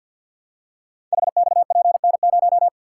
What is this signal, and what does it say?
Morse code, heard as a keyed tone, about 52 wpm, 700 Hz. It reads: HXPN0
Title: Code Groups - 5 Letter:Numbers - 50wpm